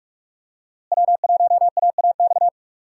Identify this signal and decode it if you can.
Morse code, heard as a keyed tone, about 46 wpm, 700 Hz. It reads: W1AAX
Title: USA Call Signs - 5 Characters - Repeated 2x - 45wpm